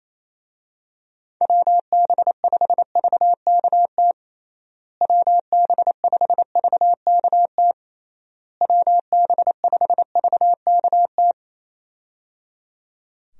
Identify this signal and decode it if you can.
Morse code, heard as a keyed tone, about 27 wpm, 700 Hz. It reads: WB5VKT WB5VKT WB5VKT
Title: USA Call Signs - 6 Characters - Repeated 3x - 28wpm